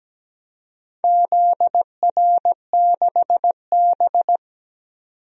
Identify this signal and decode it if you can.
Morse code, heard as a keyed tone, about 17 wpm, 700 Hz. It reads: ZR6B
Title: DXCC-100 Call Signs - 4 Characters - 17wpm